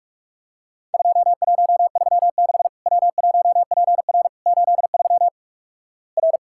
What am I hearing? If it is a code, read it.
Morse code, heard as a keyed tone, about 46 wpm, 700 Hz. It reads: 2136 W1PR 83 R
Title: Worked All Europe DX Contest - QTC Exchange - 45wpm